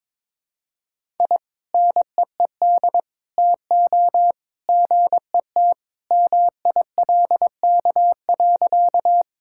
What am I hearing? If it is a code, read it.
Morse code, heard as a keyed tone, about 22 wpm, 700 Hz. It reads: I NEED TO GET MILK.
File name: practice / 22wpm / Sentences with 2-Word Phrases - 22wpm